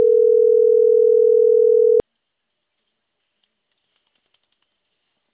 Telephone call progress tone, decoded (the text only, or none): ringback tone